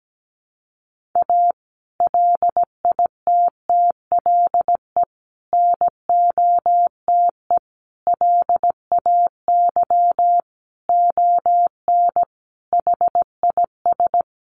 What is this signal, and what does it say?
Morse code, heard as a keyed tone, about 17 wpm, 700 Hz. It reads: A LITTLE NOTE LAY ON HIS
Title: Sentences from Top 1500 Words 17wpm